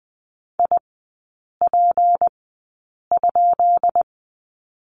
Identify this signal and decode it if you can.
Morse code, heard as a keyed tone, about 20 wpm, spaced as for 8 wpm, 700 Hz. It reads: IP?
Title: 152 Lesson 25e - Letter p - Three Characters 20wpm